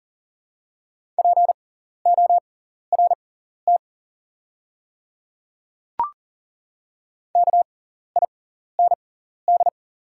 Morse code, heard as a keyed tone, about 40 wpm, spaced as for 14 wpm, 700 Hz. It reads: PORT KIND
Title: 6x Character Spacing - Top 300 Words - 40wpm